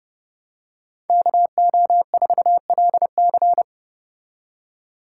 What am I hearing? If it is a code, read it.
Morse code, heard as a keyed tone, about 31 wpm, 700 Hz. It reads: KO4LC